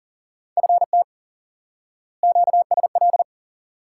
Morse code, heard as a keyed tone, about 40 wpm, 700 Hz. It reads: FT QSL
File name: IQE Course 3 - Lesson 89 - YAGI - Two Elements - 40wpm